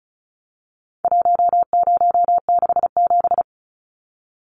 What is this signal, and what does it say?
Morse code, heard as a keyed tone, about 34 wpm, 700 Hz. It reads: 1067
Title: Numbers - 4-Digit 35wpm